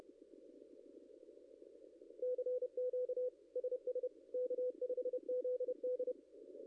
Morse code, heard as a keyed tone, about 29 wpm, 500 Hz. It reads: CQ SS K5ZD
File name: contests / ss-cw-k5tr-op-n5rz -0600